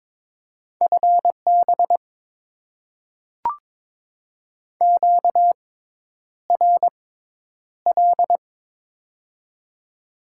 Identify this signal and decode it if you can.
Morse code, heard as a keyed tone, about 22 wpm, spaced as for 10 wpm, 700 Hz. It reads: FB QRL
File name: Top 20 QSO Elements - 6x Character Spacing - 22wpm